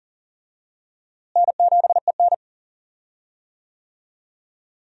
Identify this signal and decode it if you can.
Morse code, heard as a keyed tone, about 40 wpm, 700 Hz. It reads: N7EN
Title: USA Call Signs - 4 Characters - Repeated 3x - 40wpm